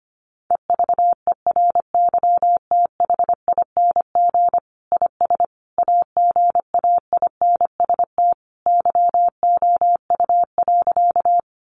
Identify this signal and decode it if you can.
Morse code, heard as a keyed tone, about 25 wpm, 700 Hz. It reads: EVERYTHING IS AGAINST YOU.
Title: Sentences from Top 700 Words 25wpm